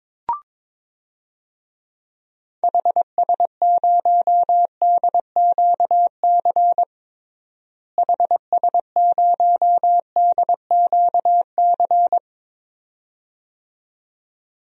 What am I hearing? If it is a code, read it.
Morse code, heard as a keyed tone, about 22 wpm, 700 Hz. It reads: HS0DQC HS0DQC
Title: DXCC-100 Call Signs - 6 Characters - Repeated 2x - 22wpm